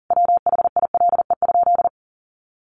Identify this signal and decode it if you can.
Morse code, heard as a keyed tone, about 40 wpm, 700 Hz. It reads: WHILE?
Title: Sentences from Top 300 Words 40wpm